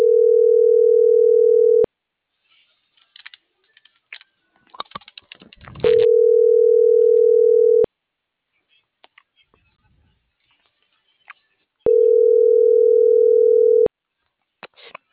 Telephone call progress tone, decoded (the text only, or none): ringback tone